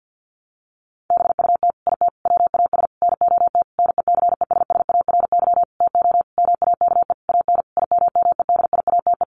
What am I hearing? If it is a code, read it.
Morse code, heard as a keyed tone, about 49 wpm, 700 Hz. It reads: BUT IT WAS NOT NECESSARY TO MAKE AN IMMEDIATE
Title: Sentences from Top 1200 Words 50wpm